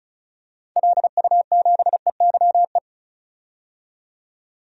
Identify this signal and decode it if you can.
Morse code, heard as a keyed tone, about 36 wpm, 700 Hz. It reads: LU7EYE
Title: DXCC-100 Call Signs - 6 Characters - 35wpm